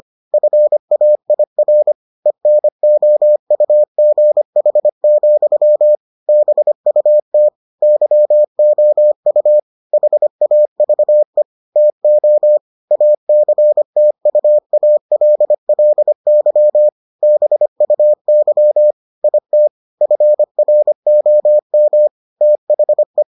Morse code, heard as a keyed tone, about 25 wpm, 600 Hz. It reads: FAIR ENOUGH, BUT YOU HAVE TO ACTUALLY BUY IT FROM THE